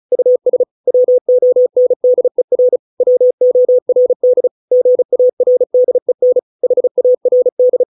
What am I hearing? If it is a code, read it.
Morse code, heard as a keyed tone, about 34 wpm, 500 Hz. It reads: US WONDER WORD GARDEN HARD